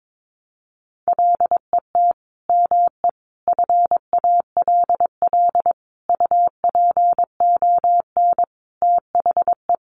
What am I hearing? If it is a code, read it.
Morse code, heard as a keyed tone, about 22 wpm, 700 Hz. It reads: LET ME FALL UPON THE
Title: Sentences from Top 1100 Words 22wpm